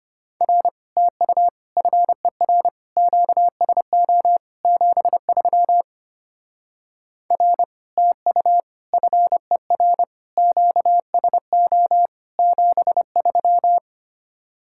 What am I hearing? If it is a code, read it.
Morse code, heard as a keyed tone, about 33 wpm, 700 Hz. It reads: R TU FER QSO 73 R TU FER QSO 73